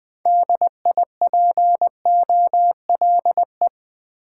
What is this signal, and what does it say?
Morse code, heard as a keyed tone, about 21 wpm, 700 Hz. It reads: DIPOLE